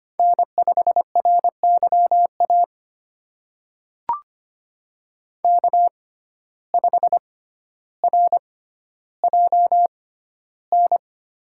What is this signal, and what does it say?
Morse code, heard as a keyed tone, about 25 wpm, spaced as for 13 wpm, 700 Hz. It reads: N5RYA K5RJN
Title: USA Call Signs - 5 Characters - 6x Character Spacing - 25wpm